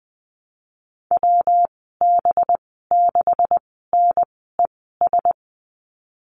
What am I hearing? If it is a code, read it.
Morse code, heard as a keyed tone, about 20 wpm, spaced as for 15 wpm, 700 Hz. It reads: WB6NES
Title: 193 Lesson 31g - Letter v - Call Signs 20wpm